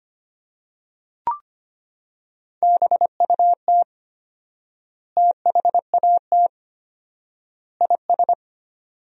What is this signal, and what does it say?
Morse code, heard as a keyed tone, about 25 wpm, 700 Hz. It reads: BUT THAT IS
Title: 4x Word-Spacing - Sentences from Top 500 Words - 25wpm